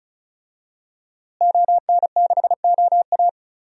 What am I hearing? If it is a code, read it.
Morse code, heard as a keyed tone, about 36 wpm, 700 Hz. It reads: ON6OA
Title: DXCC-100 Call Signs - 5 Characters - Repeated 3x - 35wpm